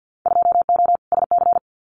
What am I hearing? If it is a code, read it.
Morse code, heard as a keyed tone, about 50 wpm, 700 Hz. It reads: 2O SC
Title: Winter Field Day Exchanges Encore 50wpm